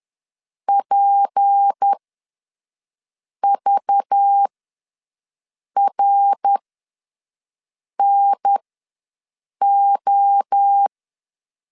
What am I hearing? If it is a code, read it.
Morse code, heard as a keyed tone, about 10 wpm, spaced as for 5 wpm, 800 Hz. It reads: PVRNO
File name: LEZIONI / 6 PKX0 / 6 PK ws 10wpm 3745